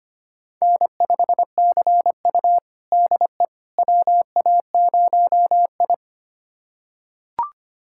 Morse code, heard as a keyed tone, about 25 wpm, 700 Hz. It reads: N5CU DE WA0I
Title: USA Callsign DE Callsign - 4 Characters - 25wpm